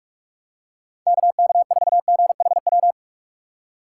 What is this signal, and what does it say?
Morse code, heard as a keyed tone, about 45 wpm, 700 Hz. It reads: KK4GHW
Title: USA Call Signs - 6 Characters - 6x Character Spacing - 45wpm